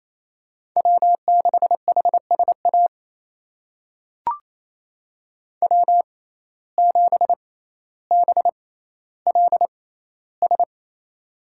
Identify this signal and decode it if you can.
Morse code, heard as a keyed tone, about 28 wpm, spaced as for 14 wpm, 700 Hz. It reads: W6HSA W7BLS
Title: USA Call Signs - 5 Characters - 6x Character Spacing - 28wpm